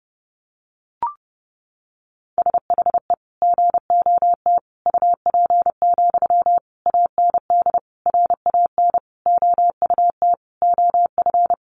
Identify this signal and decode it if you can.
Morse code, heard as a keyed tone, about 30 wpm, 700 Hz. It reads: SHE GOT UP, AND RAN OUT OF
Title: Sentences from Top 500 Words 30wpm